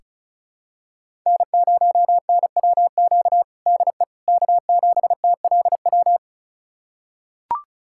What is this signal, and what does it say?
Morse code, heard as a keyed tone, about 34 wpm, 700 Hz. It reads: N0NWQ DE K7TLW
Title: USA Callsign DE Callsign - 5 Characters - 35wpm